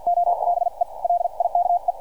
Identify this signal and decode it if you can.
Morse code, heard as a keyed tone, about 50 wpm, spaced as for 41 wpm, 700 Hz, with radio noise. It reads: OBE LIKE